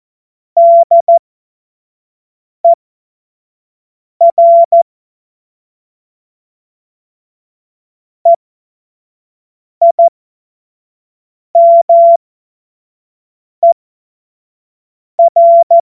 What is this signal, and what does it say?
Morse code, heard as a keyed tone, about 14 wpm, spaced as for 5 wpm, 680 Hz. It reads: DER EIMER
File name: Morsekurs / Witze / Witz 17 Juni 2025 25 F70